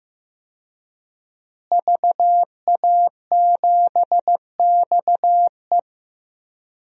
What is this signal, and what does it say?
Morse code, heard as a keyed tone, about 15 wpm, 700 Hz. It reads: VA7XE
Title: DXCC-100 Call Signs - 5 Characters - Repeated 3x - 15wpm